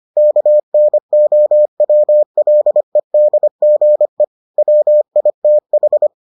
Morse code, heard as a keyed tone, about 25 wpm, 600 Hz. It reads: KNOWLEDGE WITH